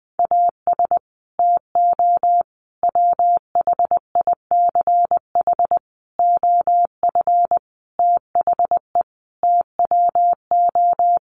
Morse code, heard as a keyed tone, about 20 wpm, 700 Hz. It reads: AS TO WHICH OF THE TWO
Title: Sentences from Top 1200 Words 20wpm